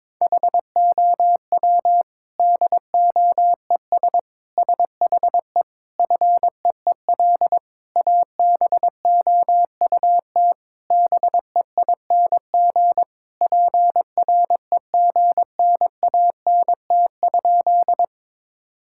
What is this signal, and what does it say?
Morse code, heard as a keyed tone, about 22 wpm, 700 Hz. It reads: HOW DOES SHE FEEL ABOUT BEING PREGNANT?
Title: Sentences from Top 1000 Words - Review Part 6 - 22wpm